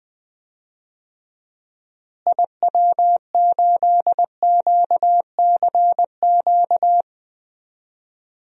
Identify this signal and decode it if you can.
Morse code, heard as a keyed tone, about 21 wpm, 700 Hz. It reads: IW8QCQ
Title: DXCC-100 Call Signs - 6 Characters - Repeated 3x - 20wpm